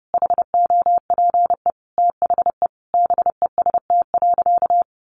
Morse code, heard as a keyed tone, about 30 wpm, 700 Hz. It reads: HOPE THE BEST.
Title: Sentences from Top 1000 Words 30wpm